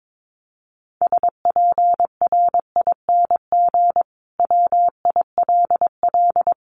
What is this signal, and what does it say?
Morse code, heard as a keyed tone, about 22 wpm, 700 Hz. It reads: SPRING WILL
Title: Sentences from Top 1200 Words 22wpm